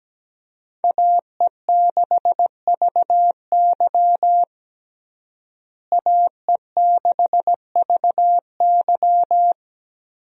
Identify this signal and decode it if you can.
Morse code, heard as a keyed tone, about 17 wpm, 700 Hz. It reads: AE6VY AE6VY
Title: USA Call Signs - 5 Characters - Repeated 3x - 17wpm